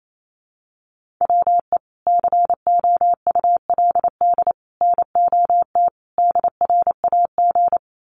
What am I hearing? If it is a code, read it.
Morse code, heard as a keyed tone, about 28 wpm, 700 Hz. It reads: WE COULD NOT DRAG
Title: Sentences from Top 1500 Words 28wpm